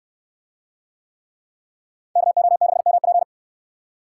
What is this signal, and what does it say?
Morse code, heard as a keyed tone, about 69 wpm, 700 Hz. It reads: KO6MP